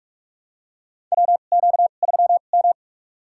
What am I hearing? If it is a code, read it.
Morse code, heard as a keyed tone, about 47 wpm, spaced as for 33 wpm, 700 Hz. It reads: WQ3M
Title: USA Call Signs - 4 Characters - 2x Character Spacing - 45wpm